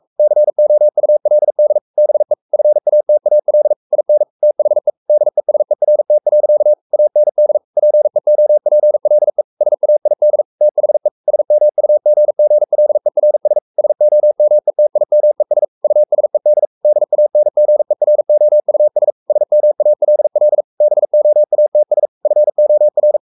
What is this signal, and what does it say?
Morse code, heard as a keyed tone, about 44 wpm, 600 Hz. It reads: COULD BE FATAL IN THE DESERT. AND PEOPLE SAID THE SMUGGLERS SOMETIMES USED DANGEROUS SMALL BOATS FOR
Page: Romanen Never af Ken Follet på engelsk.